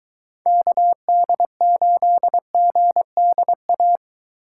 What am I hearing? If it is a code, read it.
Morse code, heard as a keyed tone, about 23 wpm, 700 Hz. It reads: KD8GDA